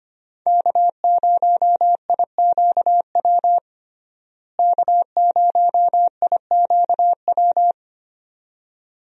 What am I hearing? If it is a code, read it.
Morse code, heard as a keyed tone, about 25 wpm, 700 Hz. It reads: K0IQW K0IQW
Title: USA Call Signs - 5 Characters - Repeated 2x - 25wpm